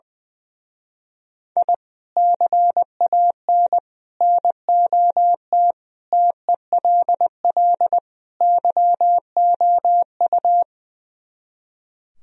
Morse code, heard as a keyed tone, about 21 wpm, 700 Hz. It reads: I CAN NOT TELL YOU
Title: Speed-Racing - Sets of 5 Words 20wpm